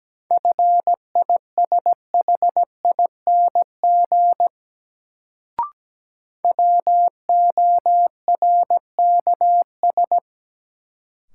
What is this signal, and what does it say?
Morse code, heard as a keyed tone, about 17 wpm, 700 Hz. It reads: FISHING WORKS
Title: Sentences from Top 1000 Words - Review Part 5 - New Word Drill - 17wpm